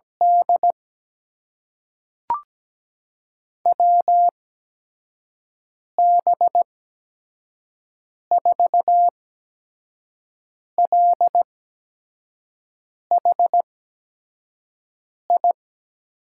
Morse code, heard as a keyed tone, about 17 wpm, spaced as for 5 wpm, 700 Hz. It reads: D WB4LHI
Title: USA Call Signs - 6 Characters - 8x Character Spacing - 17wpm